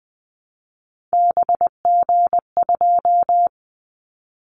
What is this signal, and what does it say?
Morse code, heard as a keyed tone, about 20 wpm, 700 Hz. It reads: BG2
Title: Callsign Prefixes for Top 100 DXCC - 20wpm